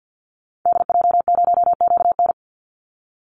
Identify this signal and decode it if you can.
Morse code, heard as a keyed tone, about 54 wpm, 700 Hz. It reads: DJ0QN
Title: Call Signs Repeated 3x - 50wpm